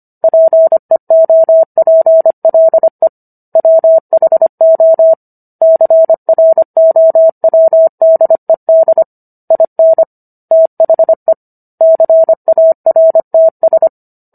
Morse code, heard as a keyed tone, about 25 wpm, 650 Hz. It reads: PEOPLE WHO CROWDED IN THE CARTS